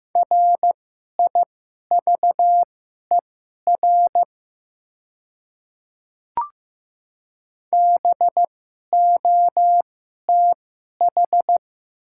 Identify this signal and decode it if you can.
Morse code, heard as a keyed tone, about 15 wpm, spaced as for 11 wpm, 700 Hz. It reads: RIVER BOTH
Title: 2x Character Spacing - Top 300 Words - 15wpm